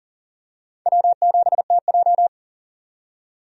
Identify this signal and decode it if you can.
Morse code, heard as a keyed tone, about 40 wpm, 700 Hz. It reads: W7TJ
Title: USA Call Signs - 4 Characters - 40wpm